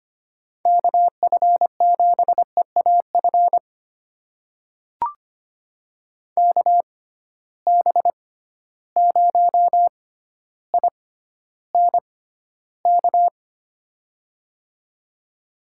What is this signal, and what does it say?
Morse code, heard as a keyed tone, about 25 wpm, spaced as for 13 wpm, 700 Hz. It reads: KF7EAF KB0INK
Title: USA Call Signs - 6 Characters - 6x Character Spacing - 25wpm